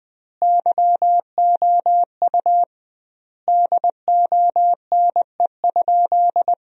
Morse code, heard as a keyed tone, about 21 wpm, 700 Hz. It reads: YOU DONE?
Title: 2x Word-Spacing - Sentences from Top 500 Words - 20wpm